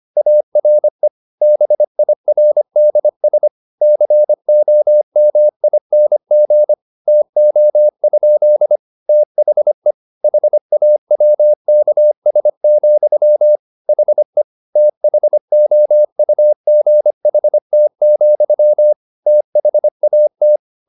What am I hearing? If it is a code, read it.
Morse code, heard as a keyed tone, about 25 wpm, 600 Hz. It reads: ARE BIRDS COMING TO? THE HAWKS, HE THOUGHT, THAT
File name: the old man and the sea part 1 25wpm